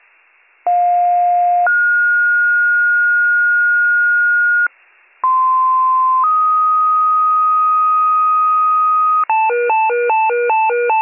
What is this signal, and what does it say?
Two-tone page, heard as a one-second tone, then a three-second tone, about 710 then 1430 Hz, about 1040 then 1250 Hz, with noise. It is heard from 0.7 to 9.2 s.